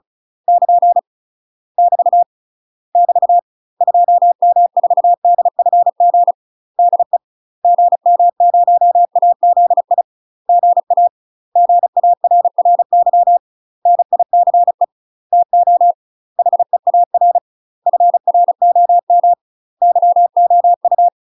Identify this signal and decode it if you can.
Morse code, heard as a keyed tone, about 34 wpm, 700 Hz. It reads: <KN> <BT> <BT> 2M4DFG DE GM0AZI GA GARRY NICE TO HEAR FROM YOU
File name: UK MASTER QSOs TRAINING / 35WPM / UK MASTER QSOs 0039